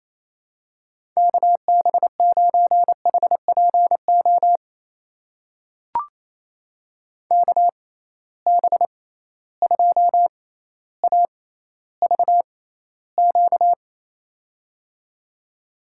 Morse code, heard as a keyed tone, about 28 wpm, spaced as for 14 wpm, 700 Hz. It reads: KB9HPO KB2AVQ
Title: USA Call Signs - 6 Characters - 6x Character Spacing - 28wpm